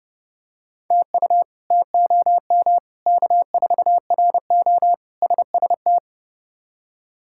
Morse code, heard as a keyed tone, about 30 wpm, 700 Hz. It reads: TU TOM K4RO SST
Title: K1USN SST QSO Practice - 30wpm